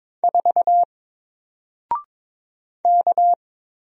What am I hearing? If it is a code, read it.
Morse code, heard as a keyed tone, about 22 wpm, 700 Hz. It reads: 4 K
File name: Single-Letter-Number-22wpm